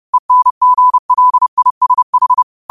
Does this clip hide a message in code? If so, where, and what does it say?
Yes: Morse code from 0.1 to 2.4 s, heard as a keyed tone, about 30 wpm, 1000 Hz. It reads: ENGLISH